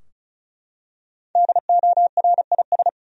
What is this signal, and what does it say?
Morse code, heard as a keyed tone, about 36 wpm, 700 Hz. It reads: DORIS
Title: USA Common Names - Max 5 Letters - 35wpm